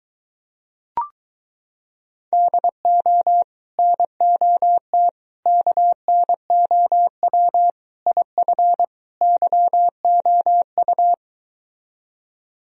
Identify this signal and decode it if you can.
Morse code, heard as a keyed tone, about 23 wpm, 700 Hz. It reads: DO NOT KNOW IF YOU